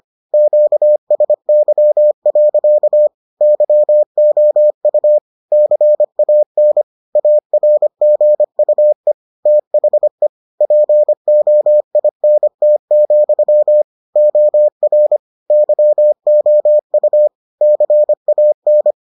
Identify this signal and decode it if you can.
Morse code, heard as a keyed tone, about 25 wpm, 600 Hz. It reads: QSY. YOU CAN ARGUE THE POINT, OR YOU CAN